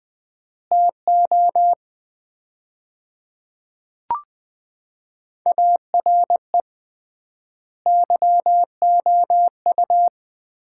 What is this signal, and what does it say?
Morse code, heard as a keyed tone, about 20 wpm, 700 Hz. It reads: TO ARE YOU
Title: 3x Word-Spacing - Sets of 4 Words - 20wpm